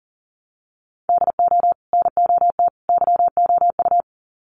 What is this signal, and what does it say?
Morse code, heard as a keyed tone, about 40 wpm, 700 Hz. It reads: DO NOT YOU
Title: Sets of 3 Words 40wpm